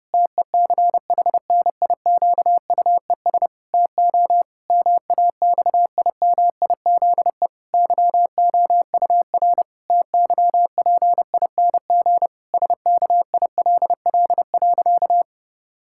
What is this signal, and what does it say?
Morse code, heard as a keyed tone, about 30 wpm, 700 Hz. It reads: TECHNIQUES TO MAXIMIZE YOUR TYPING SKILL.
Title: Pangram Challenge v2 - 30wpm